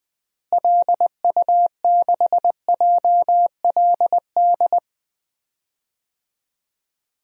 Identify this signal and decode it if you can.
Morse code, heard as a keyed tone, about 21 wpm, 700 Hz. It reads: LU6JLD
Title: DXCC-100 Call Signs - 6 Characters - Repeated 3x - 20wpm